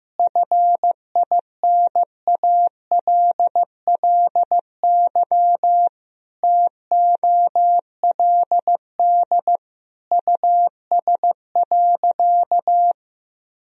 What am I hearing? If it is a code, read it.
Morse code, heard as a keyed tone, about 15 wpm, 700 Hz. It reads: FINALLY TOLD US.
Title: Sentences from Top 1000 Words - Review Part 3 - 15wpm